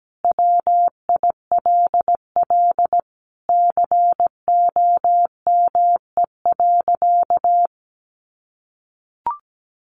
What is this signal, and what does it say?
Morse code, heard as a keyed tone, about 17 wpm, 700 Hz. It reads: WILL COME.
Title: Sentences from Top 100 Words 17wpm